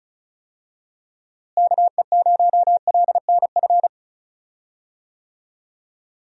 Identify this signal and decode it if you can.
Morse code, heard as a keyed tone, about 34 wpm, 700 Hz. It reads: KE0LNF
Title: USA Call Signs - 6 Characters - Repeated 3x - 35wpm